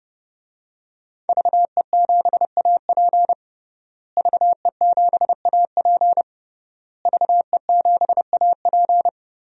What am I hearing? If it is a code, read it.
Morse code, heard as a keyed tone, about 30 wpm, 700 Hz. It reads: VE7AP VE7AP VE7AP
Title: DXCC-100 Call Signs - 5 Characters - Repeated 3x - 30wpm